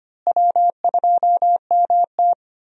Morse code, heard as a keyed tone, about 25 wpm, 700 Hz. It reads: W2MT
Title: USA Call Signs - 4 Characters - Repeated 2x - 25wpm